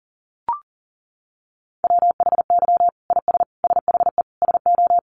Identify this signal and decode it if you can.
Morse code, heard as a keyed tone, about 40 wpm, 700 Hz. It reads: WHY IS SHE SO
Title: Sentences from Top 1200 Words 40wpm